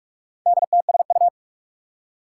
Morse code, heard as a keyed tone, about 47 wpm, 700 Hz. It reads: DTSU
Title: Four Challenging Letter-Number Combinations 45wpm